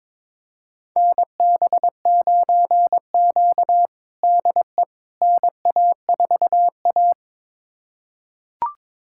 Morse code, heard as a keyed tone, about 22 wpm, 700 Hz. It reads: NB9Q DE NA4A
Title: USA Callsign DE Callsign - 4 Characters - 22wpm